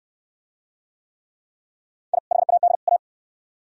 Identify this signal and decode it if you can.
Morse code, heard as a keyed tone, about 66 wpm, 700 Hz. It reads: I HAD A